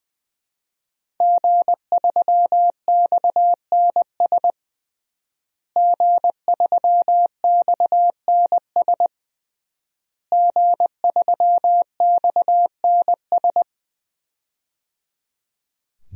Morse code, heard as a keyed tone, about 20 wpm, 700 Hz. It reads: G3XNS G3XNS G3XNS
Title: DXCC-100 Call Signs - 5 Characters - Repeated 3x - 20wpm